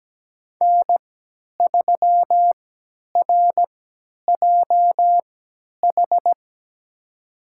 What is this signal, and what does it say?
Morse code, heard as a keyed tone, about 17 wpm, spaced as for 10 wpm, 700 Hz. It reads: N3RJH
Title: USA Call Signs - 5 Characters - 3x Character Spacing - 17wpm